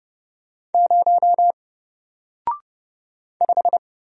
Morse code, heard as a keyed tone, about 31 wpm, 700 Hz. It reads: 0 5
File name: Single-Letter-Number-30wpm